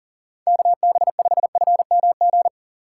Morse code, heard as a keyed tone, about 37 wpm, 700 Hz. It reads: KB5FMG
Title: USA Call Signs - 6 Characters - 2x Character Spacing - 40wpm